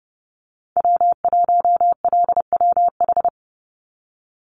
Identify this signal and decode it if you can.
Morse code, heard as a keyed tone, about 30 wpm, 700 Hz. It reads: W1LWH
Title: Call Signs Repeated 2x - 30wpm